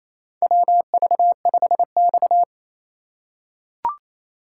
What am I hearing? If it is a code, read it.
Morse code, heard as a keyed tone, about 28 wpm, 700 Hz. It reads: WV5X
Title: USA Call Signs - 4 Characters - 2x Character Spacing - 28wpm